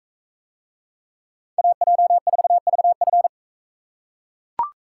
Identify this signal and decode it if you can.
Morse code, heard as a keyed tone, about 42 wpm, 700 Hz. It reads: AJ4VF